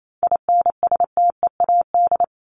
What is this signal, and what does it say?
Morse code, heard as a keyed tone, about 28 wpm, 700 Hz. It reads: INSTEAD
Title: Top 501-750 Words 28wpm